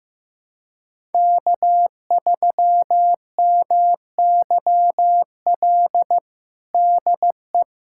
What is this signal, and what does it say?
Morse code, heard as a keyed tone, about 16 wpm, 700 Hz. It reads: K3MYL DE
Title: USA Callsign DE Callsign - 5 Characters - 15wpm